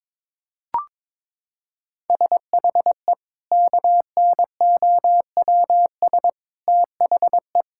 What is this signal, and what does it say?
Morse code, heard as a keyed tone, about 22 wpm, 700 Hz. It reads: SHE KNOWS THE
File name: practice / 22wpm / Sentences with 4-Word Phrases - 22wpm